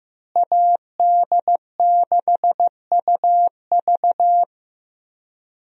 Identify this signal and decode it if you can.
Morse code, heard as a keyed tone, about 15 wpm, 700 Hz. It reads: AD6UV
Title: USA Call Signs - 5 Characters - Repeated 3x - 15wpm